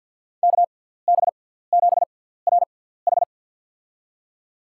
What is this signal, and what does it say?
Morse code, heard as a keyed tone, about 53 wpm, spaced as for 17 wpm, 690 Hz. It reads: KB7RH
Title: USA Call Signs - 6x Character Spacing - 50wpm